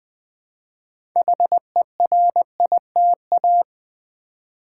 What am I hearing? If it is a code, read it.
Morse code, heard as a keyed tone, about 20 wpm, 700 Hz. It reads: HERITA
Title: Build A Word - 2 Letters At A Time - 8-Letter Words - 20wpm